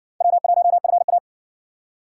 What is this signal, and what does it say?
Morse code, heard as a keyed tone, about 53 wpm, 700 Hz. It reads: W1FA